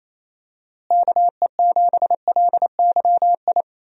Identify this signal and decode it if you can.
Morse code, heard as a keyed tone, about 28 wpm, 700 Hz. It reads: KE7LYI
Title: USA Call Signs - 6 Characters - Repeated 2x - 28wpm